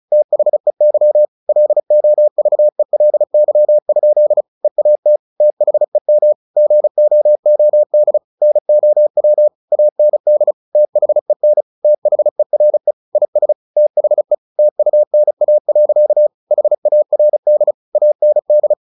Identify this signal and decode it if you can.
Morse code, heard as a keyed tone, about 34 wpm, 600 Hz. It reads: THEY LOVELY? EAT THEM GOOD NOW AND THEN THERE IS THE TUNA. HARD AND
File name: the old man and the sea / the old man and the sea 1 35wpm